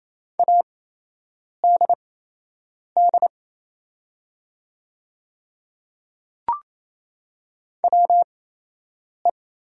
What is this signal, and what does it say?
Morse code, heard as a keyed tone, about 27 wpm, spaced as for 8 wpm, 700 Hz. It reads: ADD WE
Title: 8x Character Spacing - Top 200 Words - 28wpm